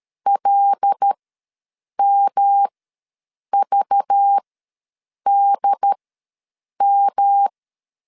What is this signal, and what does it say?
Morse code, heard as a keyed tone, about 13 wpm, spaced as for 7 wpm, 800 Hz. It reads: LMVDM
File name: LEZIONI / 5 LBVC / 5 LBV ws 12wpm 3015